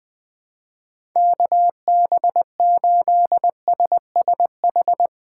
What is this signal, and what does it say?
Morse code, heard as a keyed tone, about 21 wpm, 700 Hz. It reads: KB8SSH
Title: 2023 POTA Hunter Call Signs - 20wpm